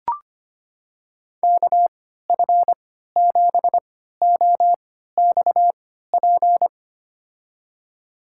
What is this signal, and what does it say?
Morse code, heard as a keyed tone, about 25 wpm, spaced as for 14 wpm, 700 Hz. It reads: KF7OXP
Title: USA Call Signs - 6 Characters - 3x Character Spacing - 25wpm